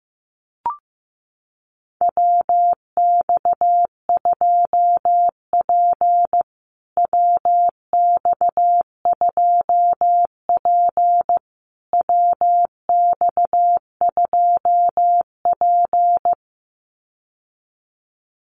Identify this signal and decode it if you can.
Morse code, heard as a keyed tone, about 15 wpm, 700 Hz. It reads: WX2P WX2P WX2P
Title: USA Call Signs Repeated 3x 15wpm